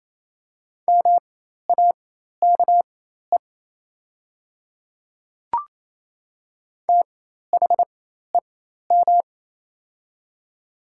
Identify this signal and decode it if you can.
Morse code, heard as a keyed tone, about 28 wpm, spaced as for 13 wpm, 690 Hz. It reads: MAKE THEM
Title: 4x Character Spacing - Top 100 Words - 28wpm